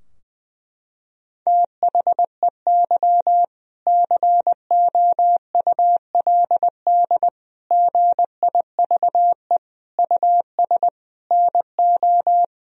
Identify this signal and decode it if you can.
Morse code, heard as a keyed tone, about 20 wpm, 700 Hz. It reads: THEY COULD GIVE US NO
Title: Sentences from Top 1000 Words - Review Part 3 - 20wpm